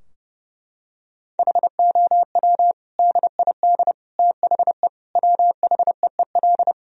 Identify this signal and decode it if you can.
Morse code, heard as a keyed tone, about 29 wpm, 700 Hz. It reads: HOW DID THE WHEEL
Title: Sentences from Top 1000 Words - Review Part 3 - 30wpm